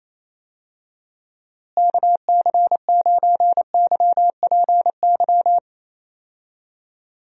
Morse code, heard as a keyed tone, about 27 wpm, 700 Hz. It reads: KC9YPY
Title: USA Call Signs - 6 Characters - Repeated 2x - 28wpm